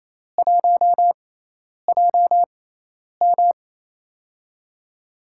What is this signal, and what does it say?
Morse code, heard as a keyed tone, about 28 wpm, spaced as for 10 wpm, 700 Hz. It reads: 1JM
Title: USA Call Signs - 4 Characters - 6x Character Spacing - 28wpm